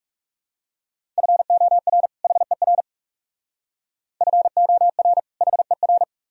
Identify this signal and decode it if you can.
Morse code, heard as a keyed tone, about 35 wpm, 700 Hz. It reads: FOR HER FOR HER
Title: Speed-Racing - Sets of 2 Words 30wpm